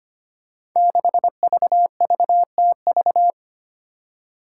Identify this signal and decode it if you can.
Morse code, heard as a keyed tone, about 25 wpm, 700 Hz. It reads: 6VVTV
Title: Five Challenging Letter-Number Combinations 25wpm